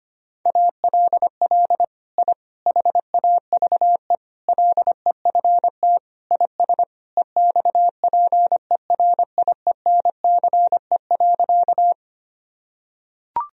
Morse code, heard as a keyed tone, about 25 wpm, 700 Hz. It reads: ALL I HAVE LEFT IS EXPERIENCE.
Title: Sentences from Top 1000 Words - Review Part 3 - 25wpm